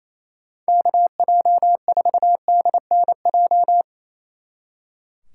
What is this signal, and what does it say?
Morse code, heard as a keyed tone, about 35 wpm, 700 Hz. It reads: KJ4DNJ
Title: Speed Racing - USA 2x3 Call Signs 28wpm